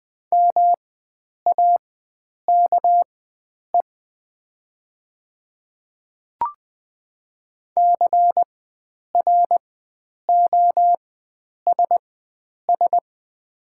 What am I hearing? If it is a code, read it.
Morse code, heard as a keyed tone, about 20 wpm, spaced as for 9 wpm, 700 Hz. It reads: MAKE CROSS
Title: 4x Character Spacing - Top 300 Words - 20wpm